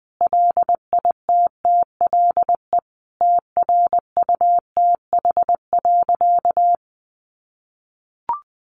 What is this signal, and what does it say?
Morse code, heard as a keyed tone, about 20 wpm, 700 Hz. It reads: LITTLE TRUTH.
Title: Sentences from Top 1300 Words 20wpm